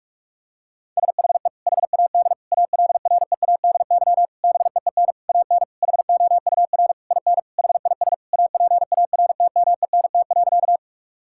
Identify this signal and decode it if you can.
Morse code, heard as a keyed tone, about 46 wpm, 700 Hz. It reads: SHE HAD ALREADY BEEN AN HOUR IN HIS APARTMENT.
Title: Sentences from Top 1000 Words - Review Part 6 - 45wpm